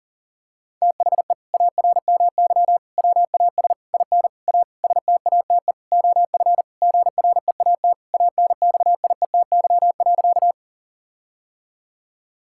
Morse code, heard as a keyed tone, about 37 wpm, 700 Hz. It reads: THE ARMY WAS IN A STATE OF GREAT ANXIETY.
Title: Sentences from Top 1600 Words 40wpm